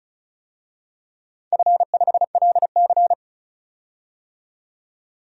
Morse code, heard as a keyed tone, about 36 wpm, 700 Hz. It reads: F5LC
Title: DXCC-100 Call Signs - 4 Characters - 35wpm